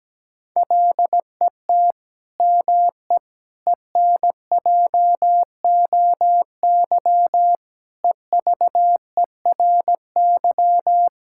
Morse code, heard as a keyed tone, about 17 wpm, 700 Hz. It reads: LET ME ENJOY EVERY
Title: Sentences from Top 1000 Words - Review Part 5 - 17wpm